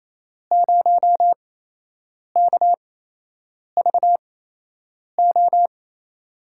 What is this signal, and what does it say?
Morse code, heard as a keyed tone, about 28 wpm, spaced as for 8 wpm, 700 Hz. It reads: 0KVO
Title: USA Call Signs - 8x Character Spacing - 28wpm